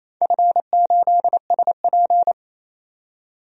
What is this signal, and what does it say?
Morse code, heard as a keyed tone, about 28 wpm, 700 Hz. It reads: F8SP
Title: DXCC-100 Call Signs - 4 Characters - 28wpm